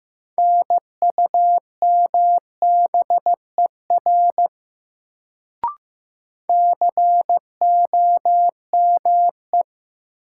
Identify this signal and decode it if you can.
Morse code, heard as a keyed tone, about 15 wpm, 700 Hz. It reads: NUMBER COME
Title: Sentences from Top 100 Words - New Word Drill - 15wpm